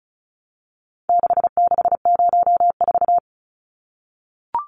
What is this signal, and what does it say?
Morse code, heard as a keyed tone, about 34 wpm, 700 Hz. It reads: 6604
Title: Numbers - 4-Digit 35wpm